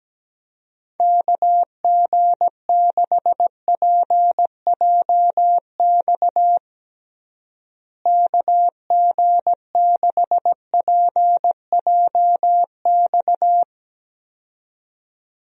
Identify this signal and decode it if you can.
Morse code, heard as a keyed tone, about 17 wpm, 700 Hz. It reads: KG6PJX KG6PJX
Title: USA Call Signs - 6 Characters - Repeated 2x - 17wpm